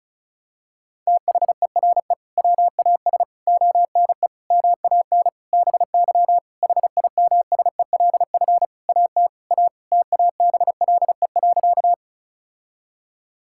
Morse code, heard as a keyed tone, about 36 wpm, 700 Hz. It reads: THERE WAS ONE MAN BY HIMSELF AT A TABLE.
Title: Sentences from Top 1600 Words 35wpm